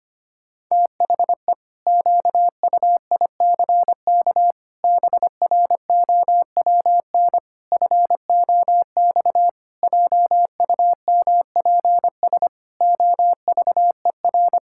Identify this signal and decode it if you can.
Morse code, heard as a keyed tone, about 25 wpm, 700 Hz. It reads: THE QUICK BROWN FOX JUMPS OVER
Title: Pangram Challenge v2 - 25wpm